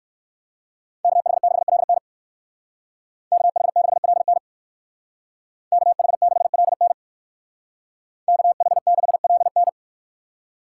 Morse code, heard as a keyed tone, about 56 wpm, 700 Hz. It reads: KH6LN KH6LN KH6LN KH6LN